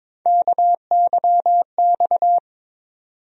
Morse code, heard as a keyed tone, about 21 wpm, 700 Hz. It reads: KYX
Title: USA Call Sign Postfixes - 3 Letters 22wpm